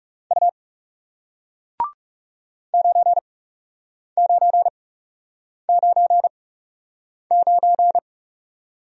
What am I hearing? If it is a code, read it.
Morse code, heard as a keyed tone, about 39 wpm, spaced as for 8 wpm, 700 Hz. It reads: U 9999